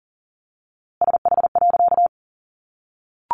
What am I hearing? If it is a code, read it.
Morse code, heard as a keyed tone, about 40 wpm, 700 Hz. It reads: SH.
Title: A Sound Of Thunder 40wpm